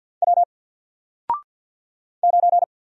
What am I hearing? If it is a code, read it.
Morse code, heard as a keyed tone, about 54 wpm, 700 Hz. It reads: W 9
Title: Single Letter-Number Alt 50wpm